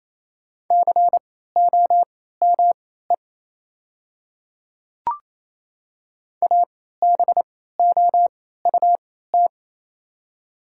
Morse code, heard as a keyed tone, about 28 wpm, spaced as for 16 wpm, 700 Hz. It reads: COME ABOUT
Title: 3x Character Spacing - Top 100 Words - 28wpm